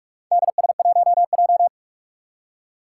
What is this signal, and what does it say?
Morse code, heard as a keyed tone, about 46 wpm, 700 Hz. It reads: DS1J
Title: Four Challenging Letter-Number Combinations 45wpm